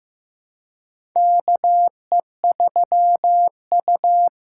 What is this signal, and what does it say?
Morse code, heard as a keyed tone, about 15 wpm, 700 Hz. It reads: KE3U
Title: USA Call Signs - 4 Characters - 3x Character Spacing - 15wpm